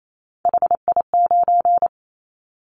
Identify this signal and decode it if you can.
Morse code, heard as a keyed tone, about 28 wpm, 700 Hz. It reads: HI9
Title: Callsign Prefixes for Top 100 DXCC - 28wpm